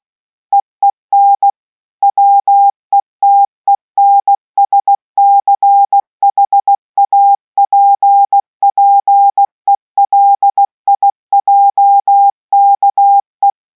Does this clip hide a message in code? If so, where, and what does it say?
Yes: Morse code from 0.5 to 13.5 s, heard as a keyed tone, about 16 wpm, 800 Hz. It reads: EEN WETENSCHAPPELIJKE